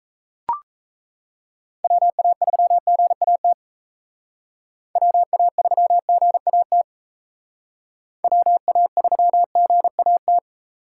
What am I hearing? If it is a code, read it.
Morse code, heard as a keyed tone, about 32 wpm, 700 Hz. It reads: WA3GAT WA3GAT WA3GAT
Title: Speed Racing - USA 2x3 Call Signs 28wpm